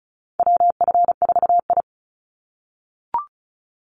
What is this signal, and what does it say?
Morse code, heard as a keyed tone, about 34 wpm, 700 Hz. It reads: WF4I
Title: NA SOTA Chaser Callsigns 35wpm